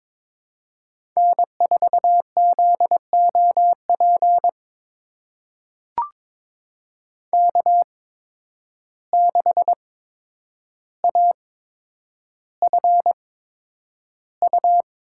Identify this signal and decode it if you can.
Morse code, heard as a keyed tone, about 22 wpm, spaced as for 9 wpm, 700 Hz. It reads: N4ZOP K6AFU
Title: USA Call Signs - 5 Characters - 8x Character Spacing - 22wpm